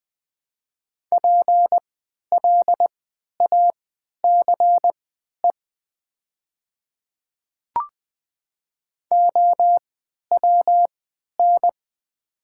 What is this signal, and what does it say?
Morse code, heard as a keyed tone, about 20 wpm, spaced as for 11 wpm, 700 Hz. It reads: PLACE OWN
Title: 3x Character Spacing - Top 300 Words - 20wpm